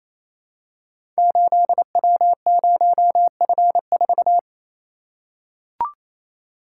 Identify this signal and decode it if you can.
Morse code, heard as a keyed tone, about 29 wpm, 700 Hz. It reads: 8W0F4
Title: Code Groups - 5 Letter:Numbers - 28wpm